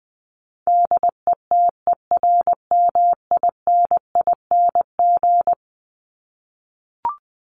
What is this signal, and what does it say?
Morse code, heard as a keyed tone, about 19 wpm, 700 Hz. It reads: DETERMINING
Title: ING Suffix - Top 100 Words with Base Words 20wpm